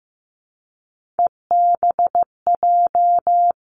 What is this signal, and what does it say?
Morse code, heard as a keyed tone, about 15 wpm, 700 Hz. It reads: EBJ
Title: Three Challenging Letter-Number Combinations 15wpm